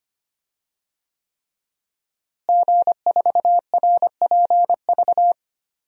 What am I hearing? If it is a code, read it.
Morse code, heard as a keyed tone, about 25 wpm, 700 Hz. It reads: G4RPV
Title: DXCC-100 Call Signs - 5 Characters - Repeated 3x - 25wpm